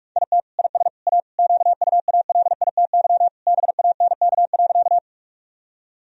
Morse code, heard as a keyed tone, about 46 wpm, 700 Hz. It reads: IT IS A QUALITY BANK.
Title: Sentences from Top 1000 Words - Review Part 4 - 45wpm